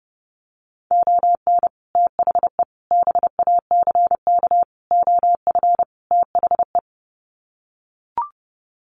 Morse code, heard as a keyed tone, about 30 wpm, 700 Hz. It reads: ON THE BACK OF THE
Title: Sets of 5 Words 30wpm